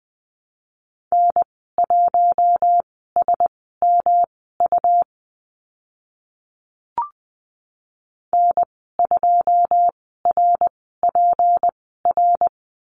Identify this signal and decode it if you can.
Morse code, heard as a keyed tone, about 20 wpm, spaced as for 15 wpm, 700 Hz. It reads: N1SMU N2RPR
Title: 167 Lesson 27g - Number 7 - Call Signs 20wpm